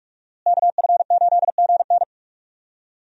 Morse code, heard as a keyed tone, about 46 wpm, 700 Hz. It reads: KF8GN
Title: USA Call Signs - 5 Characters - Repeated 2x - 45wpm